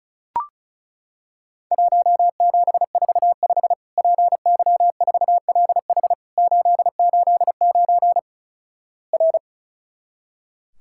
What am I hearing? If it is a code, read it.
Morse code, heard as a keyed tone, about 36 wpm, 700 Hz. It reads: 1745 PY4LH 889 R
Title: Worked All Europe DX Contest - QTC Exchange - 35wpm